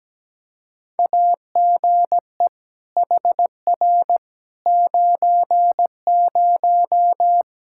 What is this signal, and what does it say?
Morse code, heard as a keyed tone, about 17 wpm, 700 Hz. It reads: AGE HR 90
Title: Practice QSO - Exchange 3 - More Info - 17wpm